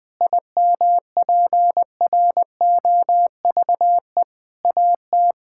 Morse code, heard as a keyed tone, about 21 wpm, 700 Hz. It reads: IMPROVE AT
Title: Sentences from Top 1000 Words - Review Part 5 - 20wpm